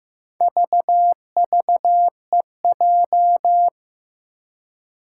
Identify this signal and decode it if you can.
Morse code, heard as a keyed tone, about 15 wpm, 700 Hz. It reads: VVEJ
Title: Four Challenging Letter-Number Combinations 15wpm